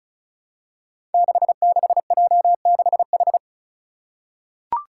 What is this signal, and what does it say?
Morse code, heard as a keyed tone, about 36 wpm, 700 Hz. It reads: 66J6H
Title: Five Challenging Letter-Number Combinations 35wpm